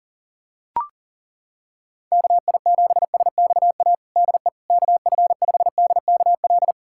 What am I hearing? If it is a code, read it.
Morse code, heard as a keyed tone, about 37 wpm, 700 Hz. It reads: KI7SXA DE KF5DKL
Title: USA Callsign DE Callsign - 6 Characters - 40wpm